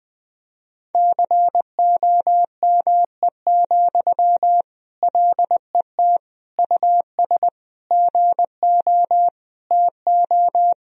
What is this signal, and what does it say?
Morse code, heard as a keyed tone, about 20 wpm, 700 Hz. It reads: COME, LET US GO TO
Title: Sentences from Top 1000 Words - Review Part 4 - 20wpm